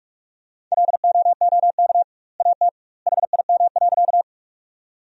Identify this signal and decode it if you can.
Morse code, heard as a keyed tone, about 41 wpm, 700 Hz. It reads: LOOK AT HIM.
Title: 2x Word-Spacing - Sentences from Top 100 Words - 45wpm